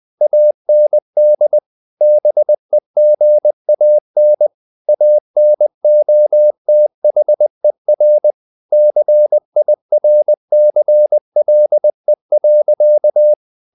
Morse code, heard as a keyed tone, about 20 wpm, 600 Hz. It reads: AND BEGAN ANOTHER CIRCLE.
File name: the old man and the sea part 2 20wpm